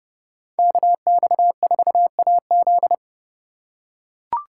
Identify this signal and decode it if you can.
Morse code, heard as a keyed tone, about 29 wpm, 700 Hz. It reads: KX4AZ